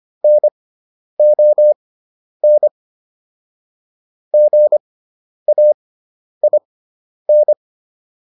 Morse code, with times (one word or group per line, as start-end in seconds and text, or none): 0.2-2.7 NON
4.3-7.5 GAIN